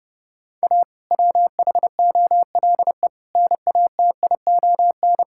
Morse code, heard as a keyed tone, about 30 wpm, 700 Hz. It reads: A WHOLE NATION
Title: Sentences from Top 1000 Words - Review Part 6 - 30wpm